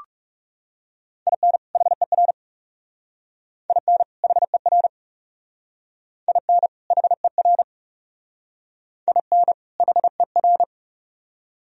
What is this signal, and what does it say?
Morse code, heard as a keyed tone, about 36 wpm, 700 Hz. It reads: IN HER IN HER IN HER IN HER
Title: Speed-Racing - Sets of 2 Words 30wpm